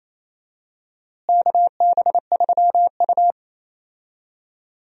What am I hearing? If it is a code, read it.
Morse code, heard as a keyed tone, about 28 wpm, 700 Hz. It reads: KB3U
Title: USA Call Signs - 4 Characters - Repeated 2x - 28wpm